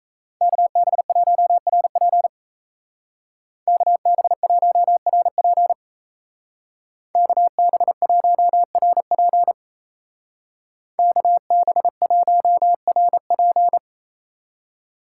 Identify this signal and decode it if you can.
Morse code, heard as a keyed tone, about 34 wpm, 700 Hz. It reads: KB1RP KB1RP KB1RP KB1RP
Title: Speed Racing - USA 2x2 Call Signs 28wpm